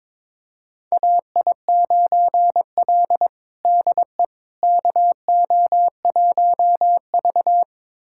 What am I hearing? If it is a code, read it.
Morse code, heard as a keyed tone, about 22 wpm, 700 Hz. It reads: AI9L DE KO1V
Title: USA Callsign DE Callsign - 4 Characters - 22wpm